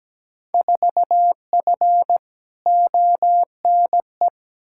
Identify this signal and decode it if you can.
Morse code, heard as a keyed tone, about 17 wpm, 700 Hz. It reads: 4F ONE
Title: Field Day Exchanges Encore - 17wpm